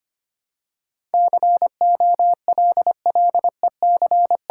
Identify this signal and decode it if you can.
Morse code, heard as a keyed tone, about 25 wpm, 700 Hz. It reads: COLLEC
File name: Ten Letter Words - 25wpm